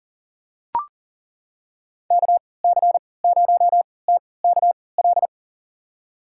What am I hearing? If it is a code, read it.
Morse code, heard as a keyed tone, about 37 wpm, spaced as for 22 wpm, 700 Hz. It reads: KC0TKL
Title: USA Call Signs - 6 Characters - 3x Character Spacing - 40wpm